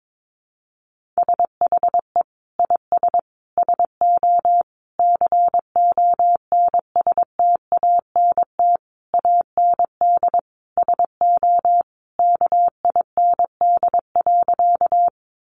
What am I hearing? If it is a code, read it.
Morse code, heard as a keyed tone, about 22 wpm, 700 Hz. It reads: SHE IS SO CONSTANT AND SO KIND.
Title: Sentences from Top 1500 Words 22wpm